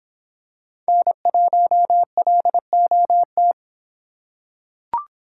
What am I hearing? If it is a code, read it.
Morse code, heard as a keyed tone, about 26 wpm, 700 Hz. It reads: N1LOT